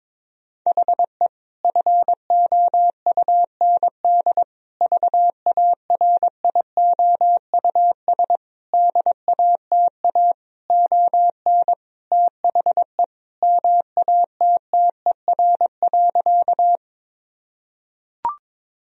Morse code, heard as a keyed tone, about 22 wpm, 700 Hz. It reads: HE FOUND VARIOUS DATA ON THE MATTER.
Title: Sentences from Top 1000 Words - Review Part 4 - 22wpm